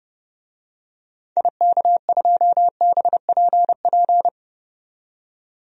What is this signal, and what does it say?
Morse code, heard as a keyed tone, about 30 wpm, 700 Hz. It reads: IK2BPP
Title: DXCC-100 Call Signs - 6 Characters - 30wpm